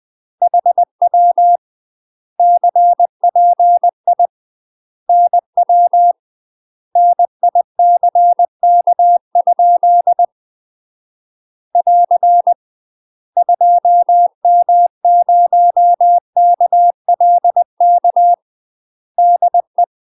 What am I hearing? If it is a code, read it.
Morse code, heard as a keyed tone, about 20 wpm, 700 Hz. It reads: HW CPI NW NICK? <AR> 2M0KLK DE
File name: UK MASTER QSOs TRAINING / 20WPM / UK MASTER QSOs 0016